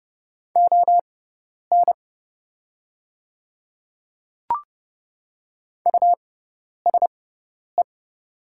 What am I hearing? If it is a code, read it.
Morse code, heard as a keyed tone, about 31 wpm, spaced as for 10 wpm, 700 Hz. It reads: ON USE
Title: 6x Character Spacing - Top 100 Words - 30wpm